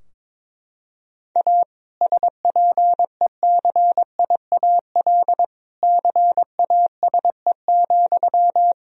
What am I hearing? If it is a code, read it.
Morse code, heard as a keyed tone, about 22 wpm, 700 Hz. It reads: A SPECIAL CASE,
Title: Sentences from Top 1000 Words - Review Part 3 - 22wpm